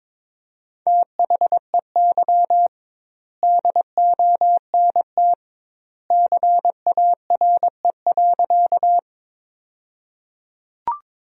Morse code, heard as a keyed tone, about 22 wpm, 700 Hz. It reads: THEY DONT CARE.
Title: 2x Word-Spacing - Sentences from Top 300 Words - 22wpm